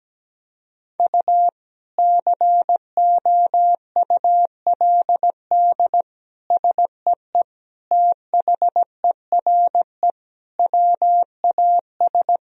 Morse code, heard as a keyed tone, about 17 wpm, 700 Hz. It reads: U COULD SEE THERE WAS
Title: Sentences from Top 500 Words and Abbreviations 17wpm